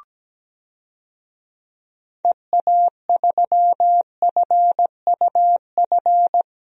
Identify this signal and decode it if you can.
Morse code, heard as a keyed tone, about 17 wpm, 700 Hz. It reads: EA3FUF
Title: DXCC-100 Call Signs - 6 Characters - Repeated 3x - 17wpm